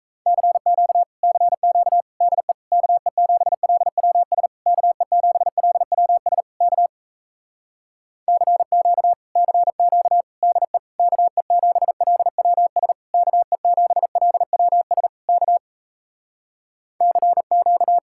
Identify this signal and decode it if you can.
Morse code, heard as a keyed tone, about 38 wpm, 700 Hz. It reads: CQ CQ DE KE7LWS KE7LWS K CQ CQ DE KE7LWS KE7LWS K CQ